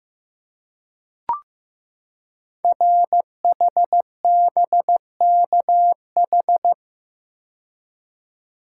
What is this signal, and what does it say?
Morse code, heard as a keyed tone, about 15 wpm, 700 Hz. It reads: RHBKH
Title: Code Groups - 5 Letters - 15wpm